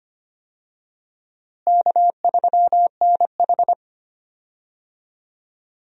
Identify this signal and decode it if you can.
Morse code, heard as a keyed tone, about 25 wpm, 700 Hz. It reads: K3NH
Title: USA Call Signs - 4 Characters - Repeated 3x - 25wpm